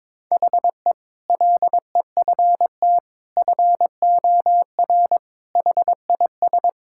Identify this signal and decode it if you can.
Morse code, heard as a keyed tone, about 22 wpm, 700 Hz. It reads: HE LEFT FOR HIS
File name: practice / 22wpm / Sentences with 2-Word Phrases - 22wpm